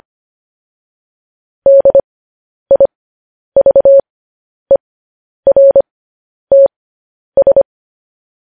Morse code, heard as a keyed tone, about 25 wpm, spaced as for 10 wpm, 550 Hz. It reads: DIVERTS